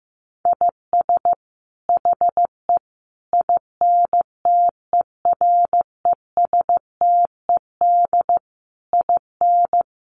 Morse code, heard as a keyed tone, about 14 wpm, 700 Hz. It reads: IS HE INTERESTED IN
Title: Sentences from Top 900 Words 15wpm